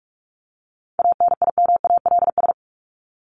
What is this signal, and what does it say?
Morse code, heard as a keyed tone, about 44 wpm, 700 Hz. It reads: ANIMALS
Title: ES:S Suffix - Top 100 Words with Base Words 45wpm